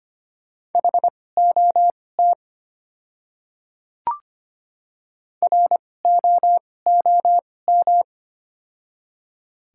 Morse code, heard as a keyed tone, about 25 wpm, spaced as for 18 wpm, 700 Hz. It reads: HOT ROOM
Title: 2x Character Spacing - Top 300 Words - 25wpm